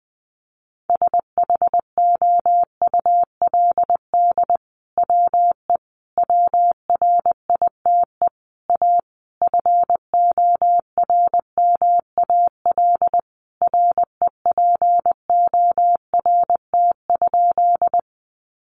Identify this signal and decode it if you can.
Morse code, heard as a keyed tone, about 20 wpm, 700 Hz. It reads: SHOULD WE WRITE A FORMAL REPORT?
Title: Sentences from Top 1300 Words 20wpm